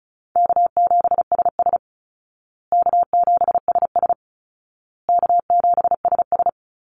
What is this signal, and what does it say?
Morse code, heard as a keyed tone, about 34 wpm, 700 Hz. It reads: K7SS K7SS K7SS
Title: Call Signs Repeated 2x - 35wpm